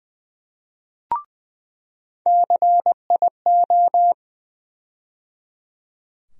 Morse code, heard as a keyed tone, about 21 wpm, 700 Hz. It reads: CIO
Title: USA Call Sign Postfixes - 3 Letters 20wpm